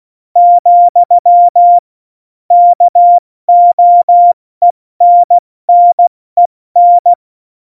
Morse code, heard as a keyed tone, about 16 wpm, 700 Hz. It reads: , KOENNEN
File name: Morsekurs / Witze / Witz 19. Juni 2018, 70 F80